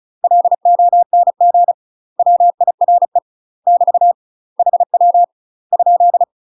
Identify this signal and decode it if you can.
Morse code, heard as a keyed tone, about 36 wpm, 700 Hz. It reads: LONG WIRE <BT> HW ?
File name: UK MASTER QSOs TRAINING / 35WPM / UK MASTER QSOs 0002